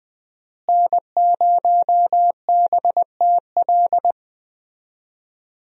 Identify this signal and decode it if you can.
Morse code, heard as a keyed tone, about 20 wpm, 700 Hz. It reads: N0BTL
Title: USA Call Signs - 5 Characters - Repeated 3x - 20wpm